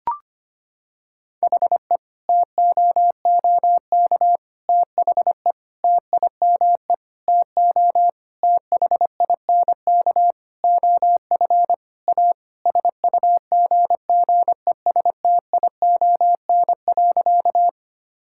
Morse code, heard as a keyed tone, about 25 wpm, 700 Hz. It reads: HE TOOK THE TIME TO THINK OF A SUGGESTION.
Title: Sentences from Top 1600 Words 25wpm